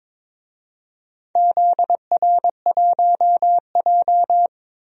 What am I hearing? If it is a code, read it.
Morse code, heard as a keyed tone, about 21 wpm, 700 Hz. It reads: ZR1J
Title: DXCC-100 Call Signs - 4 Characters - 22wpm